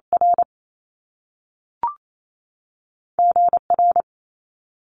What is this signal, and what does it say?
Morse code, heard as a keyed tone, about 28 wpm, 700 Hz. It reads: R G R
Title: Top Two Letter Combinations in Words 28wpm